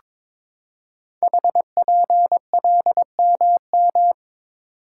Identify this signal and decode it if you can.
Morse code, heard as a keyed tone, about 22 wpm, 700 Hz. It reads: HPLMM
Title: Code Groups - 5 Letters - 22wpm